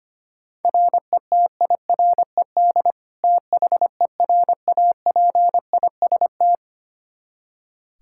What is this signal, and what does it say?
Morse code, heard as a keyed tone, about 25 wpm, 700 Hz. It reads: RETIRED THERAPIST
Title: Practice QSO - Exchange 3 - More Info - 25wpm